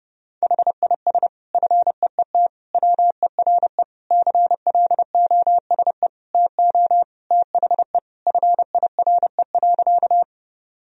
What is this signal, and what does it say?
Morse code, heard as a keyed tone, about 30 wpm, 700 Hz. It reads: HIS FEET WERE CLOSE TO THE FIRE.
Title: Sentences from Top 1000 Words - Review Part 1 - 30wpm